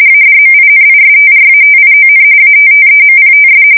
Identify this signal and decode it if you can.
RTTY, about 45.45 baud. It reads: ADMINISTRATION ATTEMPTS
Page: 170 Shift 2125/2295